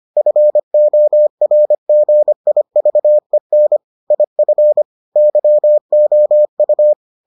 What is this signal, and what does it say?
Morse code, heard as a keyed tone, about 25 wpm, 600 Hz. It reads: FORGIVEN IF YOU